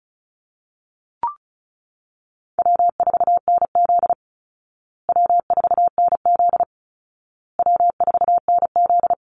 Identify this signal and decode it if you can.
Morse code, heard as a keyed tone, about 34 wpm, 700 Hz. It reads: W4NZ W4NZ W4NZ
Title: Call Signs Repeated 2x - 35wpm